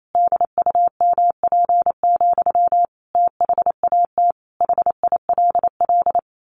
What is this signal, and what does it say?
Morse code, heard as a keyed tone, about 28 wpm, 700 Hz. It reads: DUMP, THAT HILL
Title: A Sound Of Thunder 28wpm